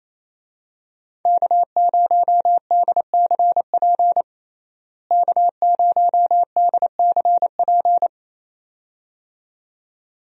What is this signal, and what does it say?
Morse code, heard as a keyed tone, about 28 wpm, 700 Hz. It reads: K0DCP K0DCP
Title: USA Call Signs - 5 Characters - Repeated 2x - 28wpm